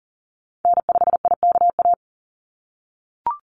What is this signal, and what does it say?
Morse code, heard as a keyed tone, about 40 wpm, 700 Hz. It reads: N5IKA
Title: USA Call Signs 40wpm